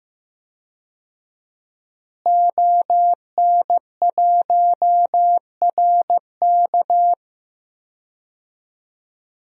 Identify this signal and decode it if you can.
Morse code, heard as a keyed tone, about 15 wpm, 700 Hz. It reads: ON1RK
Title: DXCC-100 Call Signs - 5 Characters - Repeated 3x - 15wpm